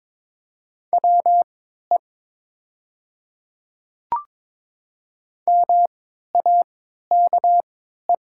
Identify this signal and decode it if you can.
Morse code, heard as a keyed tone, about 22 wpm, spaced as for 12 wpm, 700 Hz. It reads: WE MAKE
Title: 3x Character Spacing - Top 100 Words - 22wpm